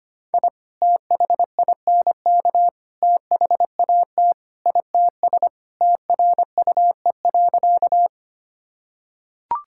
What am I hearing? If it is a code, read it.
Morse code, heard as a keyed tone, about 25 wpm, 700 Hz. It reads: I THINK THAT ITS TRUE.
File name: practice / 25wpm / Sentences with 2-Word Phrases - 25wpm